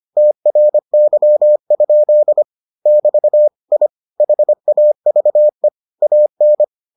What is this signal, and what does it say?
Morse code, heard as a keyed tone, about 25 wpm, 600 Hz. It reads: TRY? <BT> I HAVE AN